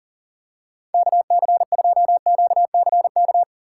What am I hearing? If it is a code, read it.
Morse code, heard as a keyed tone, about 37 wpm, 700 Hz. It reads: KC2QCK
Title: USA Call Signs - 6 Characters - 2x Character Spacing - 40wpm